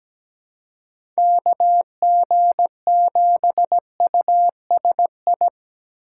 Morse code, heard as a keyed tone, about 17 wpm, 700 Hz. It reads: KG7USI
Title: USA Call Signs - 6 Characters - 3x Character Spacing - 17wpm